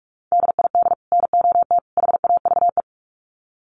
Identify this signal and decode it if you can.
Morse code, heard as a keyed tone, about 44 wpm, 700 Hz. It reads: DID NOT HAVE
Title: Sets of 3 Words 45wpm